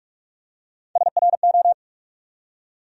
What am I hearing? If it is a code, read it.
Morse code, heard as a keyed tone, about 46 wpm, 700 Hz. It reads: SRO
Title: USA Call Sign Postfixes - 3 Letters 45wpm